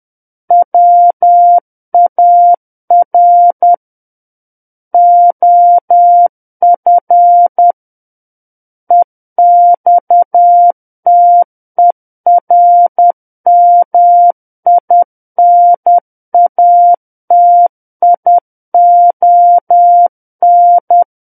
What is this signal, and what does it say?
Morse code, heard as a keyed tone, about 10 wpm, 700 Hz. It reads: WAR OF EXTERMINATION
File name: Book One / 01-Chapter 1 - 10 WPM